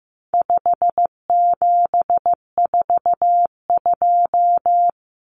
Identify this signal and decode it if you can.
Morse code, heard as a keyed tone, about 15 wpm, 700 Hz. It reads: 5742
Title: Common Cut Numbers - 4-Digit 15wpm